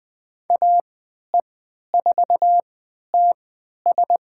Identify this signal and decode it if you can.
Morse code, heard as a keyed tone, about 20 wpm, spaced as for 11 wpm, 700 Hz. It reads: AE4TS
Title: USA Call Signs - 3x Character Spacing - 20wpm